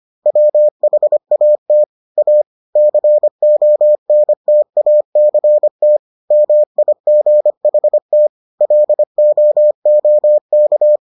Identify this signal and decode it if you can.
Morse code, heard as a keyed tone, about 25 wpm, 600 Hz. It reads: WHAT A CONTACT MIGHT LOOK